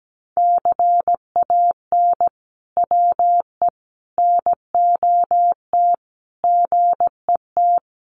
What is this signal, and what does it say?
Morse code, heard as a keyed tone, about 17 wpm, 700 Hz. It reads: CAN WE NOT GET
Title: Sentences from Top 1500 Words 17wpm